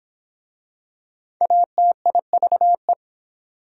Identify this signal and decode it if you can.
Morse code, heard as a keyed tone, about 25 wpm, 700 Hz. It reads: ATIVE